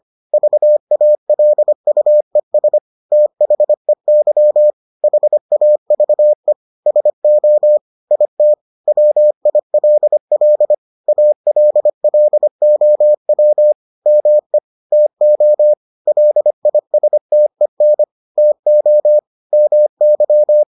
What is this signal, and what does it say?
Morse code, heard as a keyed tone, about 25 wpm, 600 Hz. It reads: VALUES THEY HAVE SO IT WILL ALLOW ME TO LISTEN TO MY